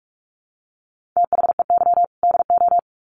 Morse code, heard as a keyed tone, about 44 wpm, 700 Hz. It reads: THEY DO
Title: Sets of 2 Words 45wpm